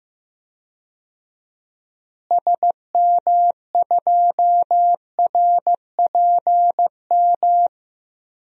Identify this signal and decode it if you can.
Morse code, heard as a keyed tone, about 15 wpm, 700 Hz. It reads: SM2RPM
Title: DXCC-100 Call Signs - 6 Characters - Repeated 3x - 15wpm